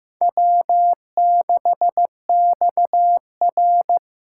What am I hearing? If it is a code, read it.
Morse code, heard as a keyed tone, about 15 wpm, 700 Hz. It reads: W6XR
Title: K1USN SST QSO Practice - 15wpm